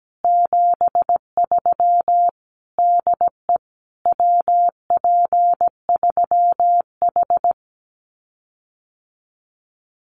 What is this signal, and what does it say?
Morse code, heard as a keyed tone, about 17 wpm, 700 Hz. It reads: 73 DE WP3H
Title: Common QSO Phrases Encore 17wpm